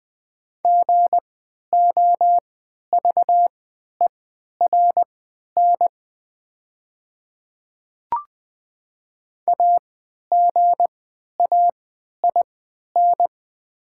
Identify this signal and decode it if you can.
Morse code, heard as a keyed tone, about 20 wpm, spaced as for 11 wpm, 700 Hz. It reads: GOVERN AGAIN
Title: 3x Character Spacing - Top 500 Words - 20wpm